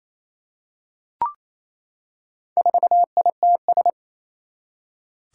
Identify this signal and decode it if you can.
Morse code, heard as a keyed tone, about 27 wpm, 700 Hz. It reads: 4ITS
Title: Four Challenging Letter-Number Combinations 28wpm